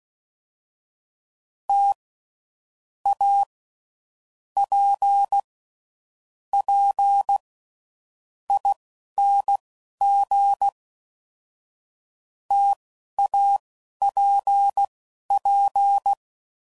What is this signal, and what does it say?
Morse code, heard as a keyed tone, about 16 wpm, spaced as for 12 wpm, 800 Hz. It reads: T A P P ING TAPP
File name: Mixed / Variable Sp / T AOEIs